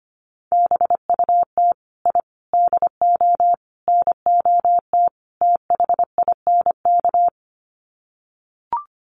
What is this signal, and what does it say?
Morse code, heard as a keyed tone, about 25 wpm, 700 Hz. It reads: BUT I DO NOT THINK
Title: Sets of 5 Words 25wpm